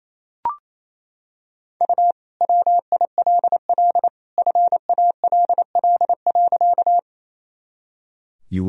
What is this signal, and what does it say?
Morse code, heard as a keyed tone, about 28 wpm, 700 Hz. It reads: U WILL FALL.
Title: Sentences from Top 500 Words and Abbreviations 28wpm